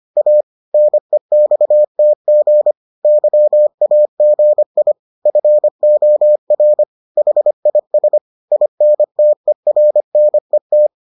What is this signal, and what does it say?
Morse code, heard as a keyed tone, about 25 wpm, 600 Hz. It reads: A NEXTG YAGI FOR HIS INTERNET